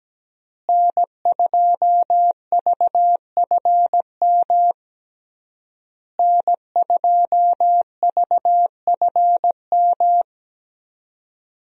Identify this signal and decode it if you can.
Morse code, heard as a keyed tone, about 17 wpm, 700 Hz. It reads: N2VFM N2VFM
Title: USA Call Signs - 5 Characters - Repeated 3x - 17wpm